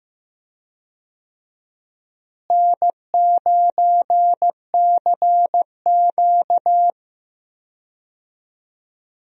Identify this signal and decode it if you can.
Morse code, heard as a keyed tone, about 15 wpm, 700 Hz. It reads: N9CQ
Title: USA Call Signs - 4 Characters - Repeated 2x - 15wpm